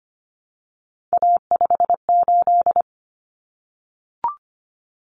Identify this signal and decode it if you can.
Morse code, heard as a keyed tone, about 25 wpm, 700 Hz. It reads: A58
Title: Common Cut Numbers - 3-Digit 25wpm